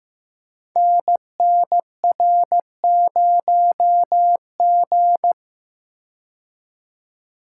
Morse code, heard as a keyed tone, about 15 wpm, 700 Hz. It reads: NNR0G
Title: Code Groups - 5 Letter:Numbers - 15wpm